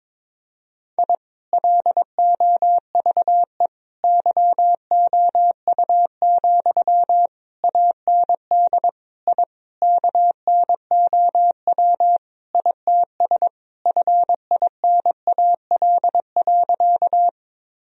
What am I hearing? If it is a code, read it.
Morse code, heard as a keyed tone, about 22 wpm, 700 Hz. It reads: I LOVE YOU, AND I KNOW ITS FINAL.
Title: Sentences from Top 1000 Words - Review Part 2 - 22wpm